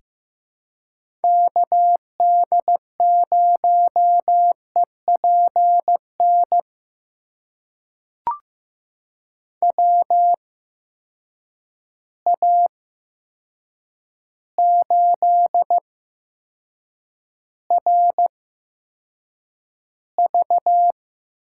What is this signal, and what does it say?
Morse code, heard as a keyed tone, about 15 wpm, spaced as for 7 wpm, 700 Hz. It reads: KD0EPN WA8RV
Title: USA Call Signs - 6 Characters - 8x Character Spacing - 15wpm